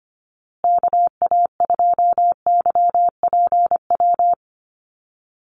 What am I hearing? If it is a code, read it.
Morse code, heard as a keyed tone, about 25 wpm, 700 Hz. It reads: KA2YPW
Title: USA Call Signs Repeated 3x 25wpm